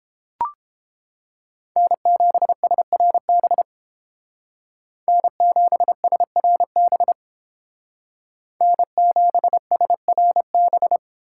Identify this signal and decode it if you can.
Morse code, heard as a keyed tone, about 30 wpm, 700 Hz. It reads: N7SRB N7SRB N7SRB